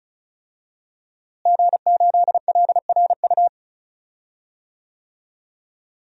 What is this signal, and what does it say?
Morse code, heard as a keyed tone, about 36 wpm, 700 Hz. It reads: G8LRU
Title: DXCC-100 Call Signs - 5 Characters - Repeated 3x - 35wpm